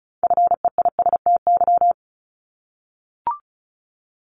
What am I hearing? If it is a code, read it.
Morse code, heard as a keyed tone, about 36 wpm, 700 Hz. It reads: FEISTY
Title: Multi-letter Phonemes 35wpm